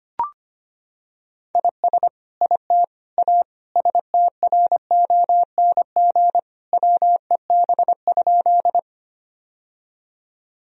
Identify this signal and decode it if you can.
Morse code, heard as a keyed tone, about 25 wpm, 700 Hz. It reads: IS IT A STRONG WEB?
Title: Sentences from Top 1000 Words - Review Part 4 - 25wpm